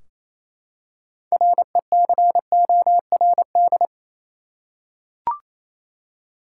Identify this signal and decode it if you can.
Morse code, heard as a keyed tone, about 28 wpm, 700 Hz. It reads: RECORD
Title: Six Letter Words - 28wpm